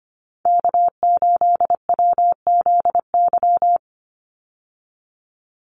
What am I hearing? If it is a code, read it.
Morse code, heard as a keyed tone, about 25 wpm, 700 Hz. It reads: K8WZY
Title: USA Call Signs 25wpm